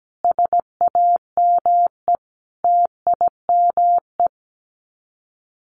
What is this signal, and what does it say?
Morse code, heard as a keyed tone, about 17 wpm, 700 Hz. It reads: SAME TIME
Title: Sets of 5 Words 17wpm